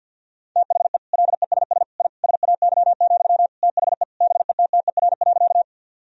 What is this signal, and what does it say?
Morse code, heard as a keyed tone, about 50 wpm, 700 Hz. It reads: THE LESS I SAY, THE BETTER.
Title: Sentences from Top 1000 Words - Review Part 2 - 50wpm